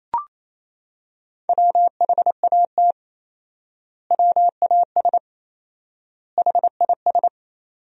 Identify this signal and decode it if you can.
Morse code, heard as a keyed tone, about 28 wpm, 700 Hz. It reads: WHAT WAS HIS
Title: 4x Word-Spacing - Sentences from Top 200 Words - 28wpm